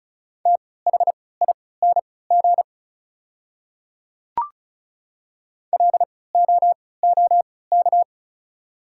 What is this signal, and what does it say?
Morse code, heard as a keyed tone, about 34 wpm, spaced as for 20 wpm, 690 Hz. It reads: THING LOOK
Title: 3x Character Spacing - Top 100 Words - 35wpm